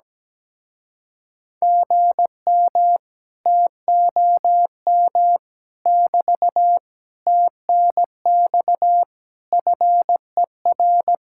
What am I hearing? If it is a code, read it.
Morse code, heard as a keyed tone, about 17 wpm, 700 Hz. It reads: GM TOM <BT> TNX FER
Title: Speed-Racing Common QSO Phrases - 17wpm